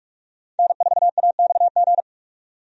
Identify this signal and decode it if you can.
Morse code, heard as a keyed tone, about 46 wpm, 700 Hz. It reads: N4AXG
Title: USA Call Signs - 5 Characters - 45wpm